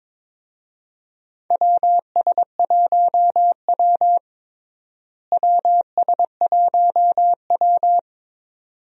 Morse code, heard as a keyed tone, about 22 wpm, 700 Hz. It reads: WS1W WS1W
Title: USA Call Signs - 4 Characters - Repeated 2x - 22wpm